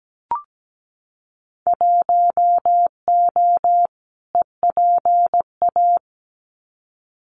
Morse code, heard as a keyed tone, about 17 wpm, 700 Hz. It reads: 1O EPA
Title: Winter Field Day Exchanges 17wpm